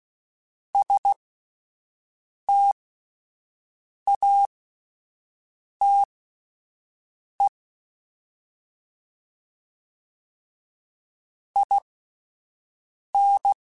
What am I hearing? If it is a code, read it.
Morse code, heard as a keyed tone, about 16 wpm, spaced as for 5 wpm, 800 Hz. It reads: STATE IN
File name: (spaced) / QSOs Spaced-mp3s / WD0BGP 1 @10 sp